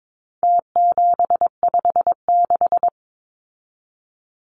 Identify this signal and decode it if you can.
Morse code, heard as a keyed tone, about 22 wpm, 700 Hz. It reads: T756
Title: Numbers - Contest Style - 4-Digit 22wpm